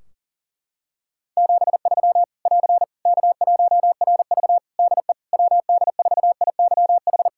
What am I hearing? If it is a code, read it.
Morse code, heard as a keyed tone, about 40 wpm, 700 Hz. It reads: 73 <AR> K1RV DE WD4IYH
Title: Practice QSO - Exchange 4 - Ending - 40wpm